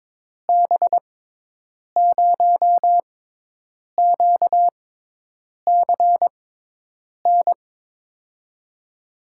Morse code, heard as a keyed tone, about 22 wpm, spaced as for 8 wpm, 700 Hz. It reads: B0QCN
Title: USA Call Signs - 6 Characters - 6x Character Spacing - 22wpm